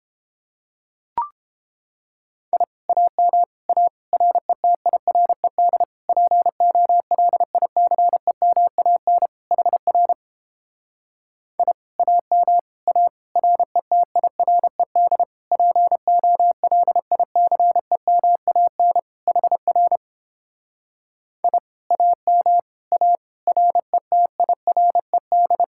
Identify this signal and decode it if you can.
Morse code, heard as a keyed tone, about 31 wpm, 700 Hz. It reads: I AM A RETIRED POLICEMAN HR I AM A RETIRED POLICEMAN HR I AM A RETIRED